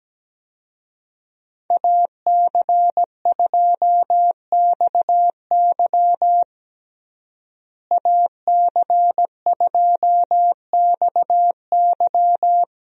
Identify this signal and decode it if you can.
Morse code, heard as a keyed tone, about 17 wpm, 700 Hz. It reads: AC2XY AC2XY
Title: USA Call Signs - 5 Characters - Repeated 2x - 17wpm